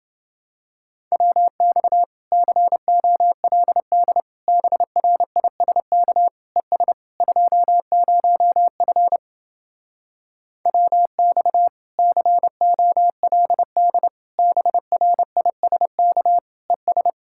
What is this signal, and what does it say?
Morse code, heard as a keyed tone, about 29 wpm, 700 Hz. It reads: WX COLD BRISK ES 20F WX COLD BRISK ES